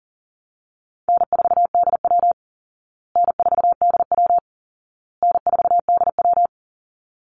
Morse code, heard as a keyed tone, about 40 wpm, 700 Hz. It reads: N4DW N4DW N4DW
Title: Easy Call Signs Repeated 3x - 40wpm